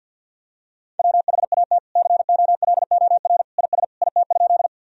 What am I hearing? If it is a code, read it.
Morse code, heard as a keyed tone, about 47 wpm, 700 Hz. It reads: WHAT COLOR IS IT?
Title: Sentences from Top 1000 Words - Review Part 2 - 50wpm